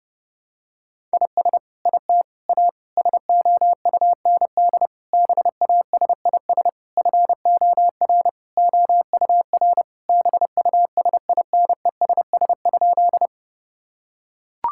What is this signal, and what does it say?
Morse code, heard as a keyed tone, about 30 wpm, 700 Hz. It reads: IS IT A SOUND BASIS FOR OUR BUSINESS?
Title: Sentences from Top 1000 Words - Review Part 5 - 30wpm